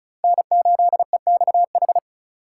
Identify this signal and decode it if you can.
Morse code, heard as a keyed tone, about 36 wpm, 700 Hz. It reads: N8EXH
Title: USA Call Signs - 4x Character Spacing - 35wpm